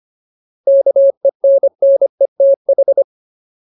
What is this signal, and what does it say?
Morse code, heard as a keyed tone, about 25 wpm, 550 Hz. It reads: KENNETH
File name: Ultimate Word List / 25WPM Ultimate